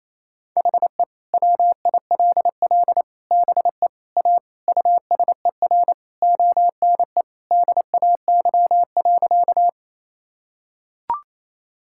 Morse code, heard as a keyed tone, about 28 wpm, 700 Hz. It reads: HE WILL BE A USER ONE DAY.
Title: Sentences from Top 1000 Words - Review Part 5 - 28wpm